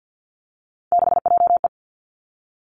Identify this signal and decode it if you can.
Morse code, heard as a keyed tone, about 50 wpm, 700 Hz. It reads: 6JE
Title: Three Challenging Letter-Number Combinations 50wpm